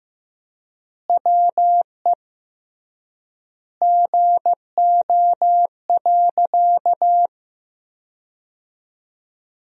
Morse code, heard as a keyed tone, about 15 wpm, 700 Hz. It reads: WE GO.
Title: 3x Word-Spacing - Sentences from Top 200 Words - 15wpm